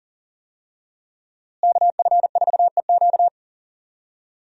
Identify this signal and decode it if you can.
Morse code, heard as a keyed tone, about 38 wpm, 700 Hz. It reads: KF4EQ
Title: USA Call Signs - 5 Characters - Repeated 2x - 40wpm